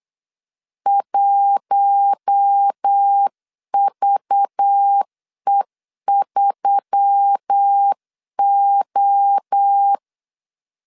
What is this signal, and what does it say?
Morse code, heard as a keyed tone, about 8 wpm, 800 Hz. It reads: 1VE3O
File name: LEZIONI / 7 GF9Z / 7 G 8wpm 3118